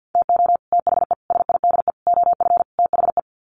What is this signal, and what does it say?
Morse code, heard as a keyed tone, about 50 wpm, 700 Hz. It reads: TO THE SIDE OF THE
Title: Sets of 5 Words 50wpm